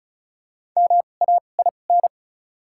Morse code, heard as a keyed tone, about 34 wpm, spaced as for 25 wpm, 700 Hz. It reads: MAIN
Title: 2x Character Spacing - Top 500 Words - 35wpm